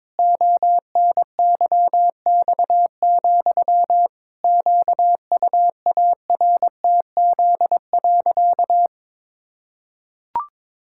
Morse code, heard as a keyed tone, about 22 wpm, 700 Hz. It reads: ONYX, QUARTZ.
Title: Pangram Challenge v2 - 22wpm